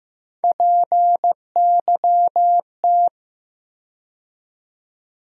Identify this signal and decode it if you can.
Morse code, heard as a keyed tone, about 15 wpm, 700 Hz. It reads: PYT
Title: USA Call Sign Postfixes - 3 Letters 15wpm